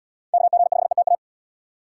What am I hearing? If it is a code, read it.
Morse code, heard as a keyed tone, about 74 wpm, 700 Hz. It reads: KG5ETA